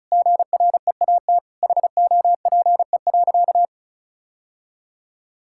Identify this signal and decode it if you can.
Morse code, heard as a keyed tone, about 36 wpm, 700 Hz. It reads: GREAT HOPE.
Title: Sentences from Top 1000 Words - Review Part 5 - 35wpm